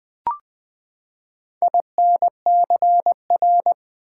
Morse code, heard as a keyed tone, about 20 wpm, 700 Hz. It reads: INCR
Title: Build A Word - 4 Letters At A Time - 12-Letter Words - 20wpm